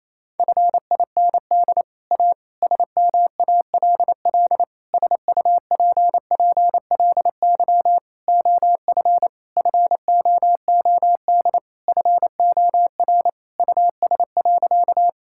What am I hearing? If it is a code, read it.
Morse code, heard as a keyed tone, about 28 wpm, 700 Hz. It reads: FIND A SMALL SUPPLY OF FOOD FOR US.
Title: Sentences from Top 1000 Words - Review Part 5 - 28wpm